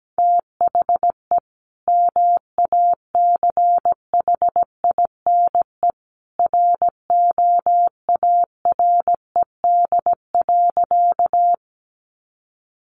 Morse code, heard as a keyed tone, about 17 wpm, 700 Hz. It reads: THE MACHINE ROARED.
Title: A Sound Of Thunder 17wpm